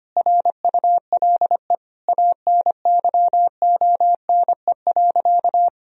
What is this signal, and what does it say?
Morse code, heard as a keyed tone, about 25 wpm, 700 Hz. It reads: RULE ANYONE.
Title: Sentences from Top 1000 Words - Review Part 2 - 25wpm